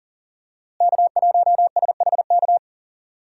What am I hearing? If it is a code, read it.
Morse code, heard as a keyed tone, about 39 wpm, 700 Hz. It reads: K1SHK
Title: USA Call Signs - 5 Characters - Repeated 2x - 40wpm